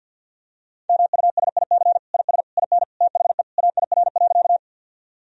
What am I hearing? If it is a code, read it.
Morse code, heard as a keyed tone, about 47 wpm, 700 Hz. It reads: MUSIC IS IN THE AIR.
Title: Sentences from Top 1000 Words - Review Part 1 - 50wpm